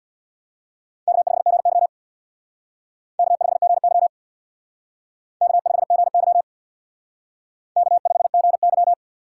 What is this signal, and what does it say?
Morse code, heard as a keyed tone, about 61 wpm, 700 Hz. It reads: K5GY K5GY K5GY K5GY
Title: Speed Racing - USA 1x2 Call Signs 50wpm